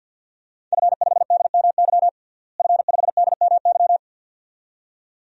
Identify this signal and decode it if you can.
Morse code, heard as a keyed tone, about 49 wpm, 700 Hz. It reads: F5DMY F5DMY
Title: DXCC-100 Call Signs - 5 Characters - Repeated 2x - 50wpm